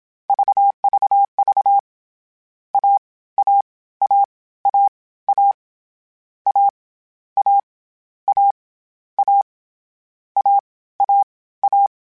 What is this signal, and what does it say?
Morse code, heard as a keyed tone, about 27 wpm, 800 Hz. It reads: VVV A A A A A A A A A A A A
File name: Difficult Combos / Difficult Combos-mp3s / AWJ1 @25